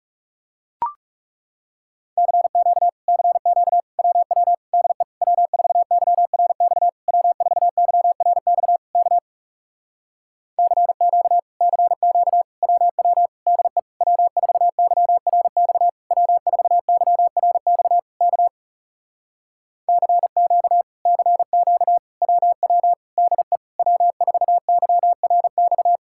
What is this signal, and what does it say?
Morse code, heard as a keyed tone, about 40 wpm, 700 Hz. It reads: CQ CQ WW DE W4YRX W4YRX K CQ CQ WW DE W4YRX W4YRX K CQ CQ WW DE W4YRX